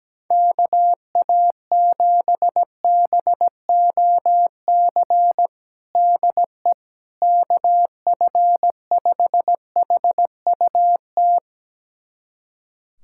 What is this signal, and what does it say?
Morse code, heard as a keyed tone, about 17 wpm, 700 Hz. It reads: KA7BOC DE KF5HUT
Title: USA Callsign DE Callsign - 6 Characters - 17wpm